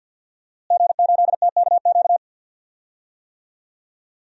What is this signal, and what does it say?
Morse code, heard as a keyed tone, about 48 wpm, 700 Hz. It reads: G8TKQ
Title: DXCC-100 Call Signs - 5 Characters - Repeated 2x - 50wpm